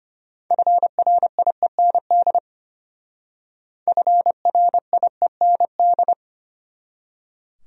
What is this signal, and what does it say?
Morse code, heard as a keyed tone, about 31 wpm, 700 Hz. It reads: FRIEND FRIEND
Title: Speed-Racing Top 500 Words 25wpm